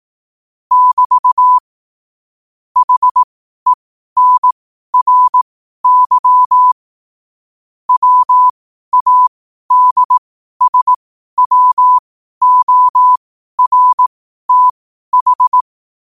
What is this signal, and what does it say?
Morse code, heard as a keyed tone, about 18 wpm, spaced as for 12 wpm, 1000 Hz. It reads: <BT> HENRY WADSWORTH
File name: qotdfiles / 12wpm